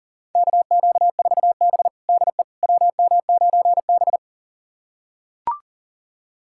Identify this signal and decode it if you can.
Morse code, heard as a keyed tone, about 40 wpm, 700 Hz. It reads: KQ4B DE WM9B
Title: USA Callsign DE Callsign - 4 Characters - 40wpm